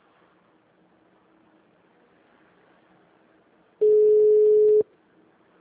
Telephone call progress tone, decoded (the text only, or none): ringback tone